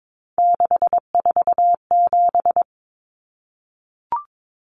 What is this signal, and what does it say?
Morse code, heard as a keyed tone, about 22 wpm, 700 Hz. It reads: 647
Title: Numbers - 3-Digit 22wpm